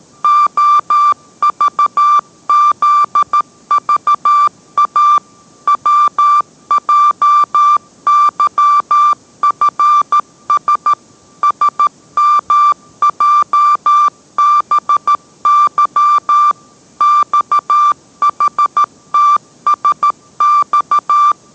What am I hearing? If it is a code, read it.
Morse code, heard as a keyed tone, about 13 wpm, 1180 Hz. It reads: OVZVA WJYFS SMJBY XHTSX